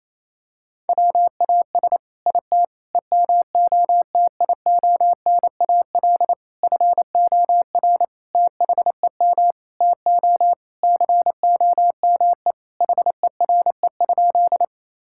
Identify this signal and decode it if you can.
Morse code, heard as a keyed tone, about 28 wpm, 700 Hz. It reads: WAS IT EMOTIONAL FOR THEM TO COME HERE?
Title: Sentences from Top 1000 Words - Review Part 5 - 28wpm